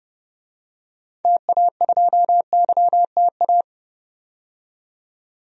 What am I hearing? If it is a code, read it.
Morse code, heard as a keyed tone, about 30 wpm, 700 Hz. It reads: TA2YTA
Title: DXCC-100 Call Signs - 6 Characters - 30wpm